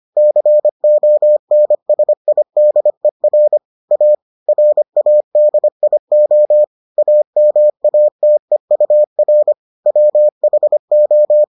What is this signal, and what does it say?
Morse code, heard as a keyed tone, about 25 wpm, 600 Hz. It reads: CONSIDER A RADIO AMATEUR WHO